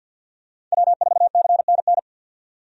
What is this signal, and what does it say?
Morse code, heard as a keyed tone, about 46 wpm, 700 Hz. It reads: W4CNN
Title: USA Call Signs - 5 Characters - 50wpm